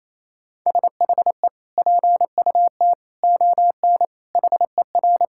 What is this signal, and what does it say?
Morse code, heard as a keyed tone, about 28 wpm, 700 Hz. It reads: SHE PUT ON HER
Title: Sentences from Top 1600 Words 28wpm